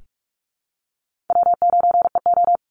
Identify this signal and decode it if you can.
Morse code, heard as a keyed tone, about 44 wpm, 700 Hz. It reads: W9EO
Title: Call Signs Repeated 2x - 45wpm